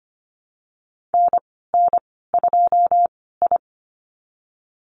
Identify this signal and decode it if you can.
Morse code, heard as a keyed tone, about 25 wpm, spaced as for 16 wpm, 700 Hz. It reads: NN2I
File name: Lesson 15g - Number 5 - Call Signs 25wpm